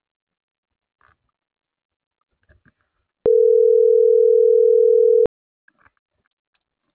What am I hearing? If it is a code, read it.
Ringback tone (North America), 440 + 480 Hz.